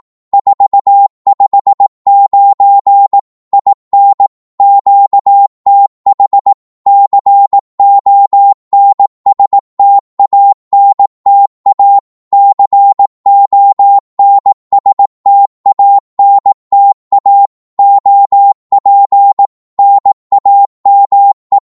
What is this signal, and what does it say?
Morse code, heard as a keyed tone, about 18 wpm, 800 Hz. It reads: 459 IN QTH CONSTANTA CONSTANTA OP NAME
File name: Morse / qso-18wpm